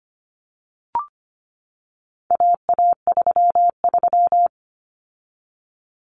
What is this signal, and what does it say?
Morse code, heard as a keyed tone, about 25 wpm, 700 Hz. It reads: AA33
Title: Common Cut Numbers - 4-Digit 25wpm